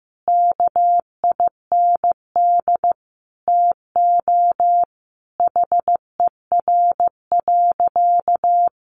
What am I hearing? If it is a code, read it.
Morse code, heard as a keyed tone, about 15 wpm, 700 Hz. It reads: KIND TO HER.
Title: Sentences from Top 700 Words 15wpm